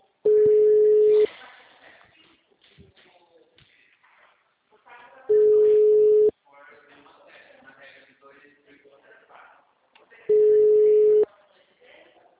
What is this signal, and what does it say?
Ringback tone (Europe, 425 Hz), with speech.
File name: out- -2006- - - .11244